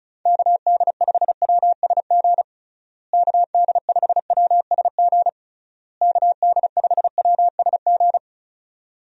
Morse code, heard as a keyed tone, about 34 wpm, 700 Hz. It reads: KD5WSG KD5WSG KD5WSG
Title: USA Call Signs - 6 Characters - Repeated 3x - 35wpm